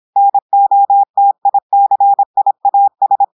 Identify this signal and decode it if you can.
Morse code, heard as a keyed tone, about 26 wpm, 800 Hz. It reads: NOTICIAS
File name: portugais texte en clair 2 vitesse 26 mots minutes